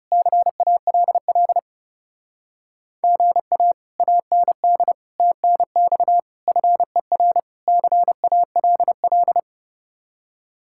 Morse code, heard as a keyed tone, about 39 wpm, 700 Hz. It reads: CALL GA AND TNX FER CALL